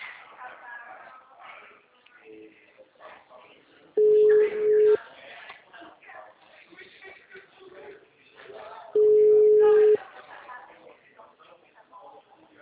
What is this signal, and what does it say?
Ringback tone (Europe, 425 Hz), with speech.